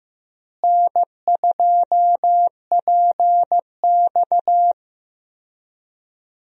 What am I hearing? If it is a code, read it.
Morse code, heard as a keyed tone, about 15 wpm, 700 Hz. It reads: N2PX
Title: USA Call Signs - 4 Characters - Repeated 3x - 15wpm